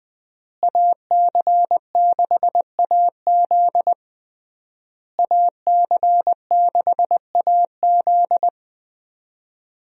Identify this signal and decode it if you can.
Morse code, heard as a keyed tone, about 20 wpm, 700 Hz. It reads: AC6AZ AC6AZ
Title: USA Call Signs - 5 Characters - Repeated 3x - 20wpm